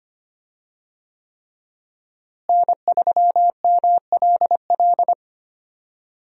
Morse code, heard as a keyed tone, about 25 wpm, 700 Hz. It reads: N3MLL
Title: USA Call Signs - 5 Characters - Repeated 3x - 25wpm